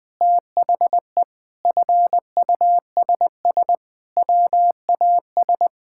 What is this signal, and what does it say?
Morse code, heard as a keyed tone, about 20 wpm, 700 Hz. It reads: THE FUSS WAS
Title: Sentences with 8-Word Phrases - 20wpm